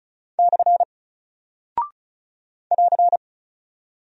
Morse code, heard as a keyed tone, about 36 wpm, 700 Hz. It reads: / <AR>
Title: Single Character (Letter:Number:Punctuation:Prosign) 35wpm